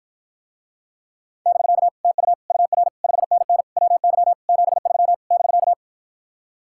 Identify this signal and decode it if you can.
Morse code, heard as a keyed tone, about 54 wpm, 700 Hz. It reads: <BK> TU UR 5NN WY 73 <BK>